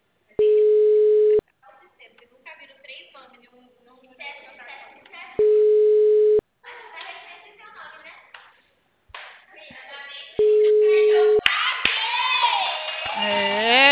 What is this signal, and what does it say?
Ringback tone (Europe, 425 Hz), with speech.